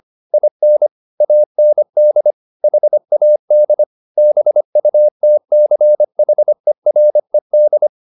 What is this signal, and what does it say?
Morse code, heard as a keyed tone, about 25 wpm, 600 Hz. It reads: IN AND HAD BUTCHERED
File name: the old man and the sea part 1 25wpm